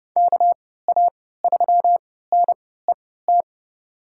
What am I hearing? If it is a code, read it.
Morse code, heard as a keyed tone, about 30 wpm, spaced as for 17 wpm, 700 Hz. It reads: KA3NET
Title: USA Call Signs - 6 Characters - 3x Character Spacing - 30wpm